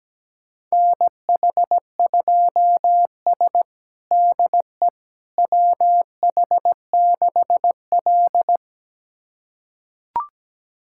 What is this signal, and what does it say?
Morse code, heard as a keyed tone, about 17 wpm, 700 Hz. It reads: NH2S DE WH6L
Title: USA Callsign DE Callsign - 4 Characters - 17wpm